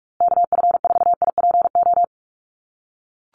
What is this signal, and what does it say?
Morse code, heard as a keyed tone, about 44 wpm, 700 Hz. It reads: KF4IPO
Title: USA Call Signs Repeated 2x 45wpm